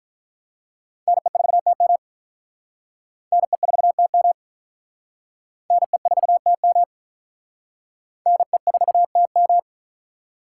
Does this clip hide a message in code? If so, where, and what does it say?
Yes: Morse code from 1.1 to 9.6 s, heard as a keyed tone, about 46 wpm, 700 Hz. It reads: NE4TM NE4TM NE4TM NE4TM